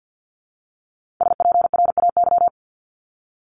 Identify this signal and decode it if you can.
Morse code, heard as a keyed tone, about 50 wpm, 700 Hz. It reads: SPRAY
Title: Multi-letter Phonemes Repeated 3x 50wpm